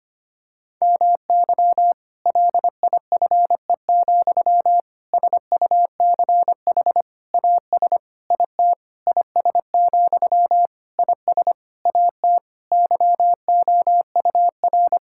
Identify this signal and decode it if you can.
Morse code, heard as a keyed tone, about 25 wpm, 700 Hz. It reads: MY LIFE, SUCH AS IT IS, IS AT YOUR
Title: Sentences from Top 1600 Words 25wpm